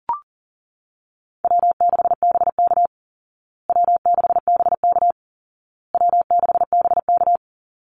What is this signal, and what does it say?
Morse code, heard as a keyed tone, about 40 wpm, 700 Hz. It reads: W6BK W6BK W6BK
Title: Easy Call Signs Repeated 3x - 40wpm